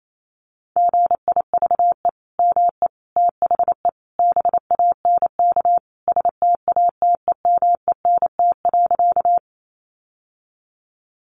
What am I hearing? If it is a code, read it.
Morse code, heard as a keyed tone, about 28 wpm, 700 Hz. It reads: GIVE ME THE BANK STATEMENT.
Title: Sentences from Top 1200 Words 28wpm